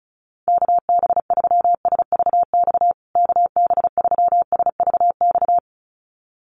Morse code, heard as a keyed tone, about 34 wpm, 700 Hz. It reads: KB3SVX KB3SVX
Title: USA Call Signs Repeated 2x 35wpm